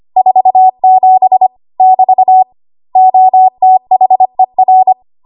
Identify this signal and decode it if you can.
Morse code, heard as a keyed tone, about 25 wpm, 750 Hz. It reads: 47 <BT> OTHER
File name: Archive / 25 WPM / 25WPM